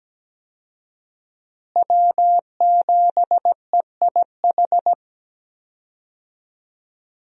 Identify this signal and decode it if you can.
Morse code, heard as a keyed tone, about 17 wpm, 700 Hz. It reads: W7EIH
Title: USA Call Signs - 5 Characters - Repeated 2x - 17wpm